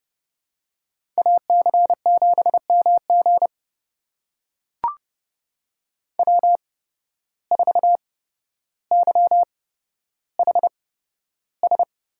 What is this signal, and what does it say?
Morse code, heard as a keyed tone, about 32 wpm, spaced as for 13 wpm, 700 Hz. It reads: AC7MG W4YHS
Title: USA Call Signs - 8x Character Spacing - 30wpm